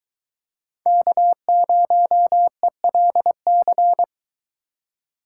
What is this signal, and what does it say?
Morse code, heard as a keyed tone, about 23 wpm, 700 Hz. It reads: K0ELC